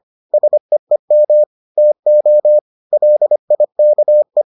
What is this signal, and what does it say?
Morse code, heard as a keyed tone, about 25 wpm, 600 Hz. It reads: SEEM TO LIKE